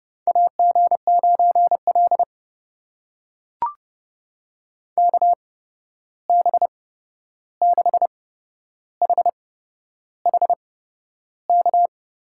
Morse code, heard as a keyed tone, about 30 wpm, spaced as for 11 wpm, 700 Hz. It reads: AG9L KB6HHK
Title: USA Call Signs - 8x Character Spacing - 30wpm